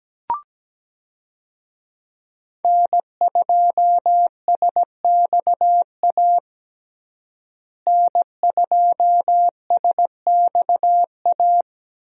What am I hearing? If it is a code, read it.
Morse code, heard as a keyed tone, about 17 wpm, 700 Hz. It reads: N2SXA N2SXA
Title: USA Call Signs - 5 Characters - Repeated 2x - 17wpm